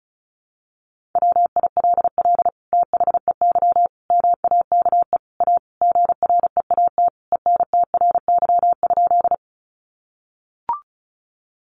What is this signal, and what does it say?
Morse code, heard as a keyed tone, about 34 wpm, 700 Hz. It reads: WILL THEY MAKE A GREAT ENTRY?
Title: Sentences from Top 1500 Words 35wpm